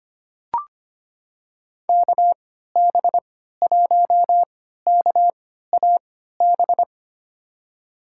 Morse code, heard as a keyed tone, about 25 wpm, spaced as for 14 wpm, 700 Hz. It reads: KB1KAB
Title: USA Call Signs - 6 Characters - 3x Character Spacing - 25wpm